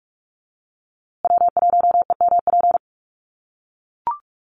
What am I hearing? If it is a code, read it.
Morse code, heard as a keyed tone, about 44 wpm, 700 Hz. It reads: W1EMP
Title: USA Call Signs Repeated 3x 45wpm